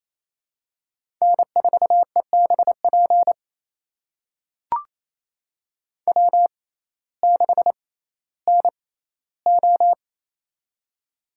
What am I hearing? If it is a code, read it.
Morse code, heard as a keyed tone, about 28 wpm, spaced as for 15 wpm, 700 Hz. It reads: N4EBP W6NO
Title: USA Call Signs - 6x Character Spacing - 28wpm